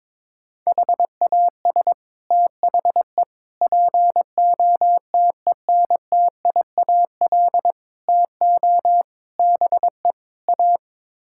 Morse code, heard as a keyed tone, about 22 wpm, 700 Hz. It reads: HAS THE POTENTIAL TO BE A
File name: practice / 22wpm / Sets of 6 Words - 22wpm